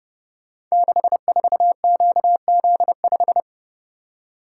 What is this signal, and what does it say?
Morse code, heard as a keyed tone, about 28 wpm, 700 Hz. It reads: 64QZ5
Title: Code Groups - 5 Letter:Numbers - 30wpm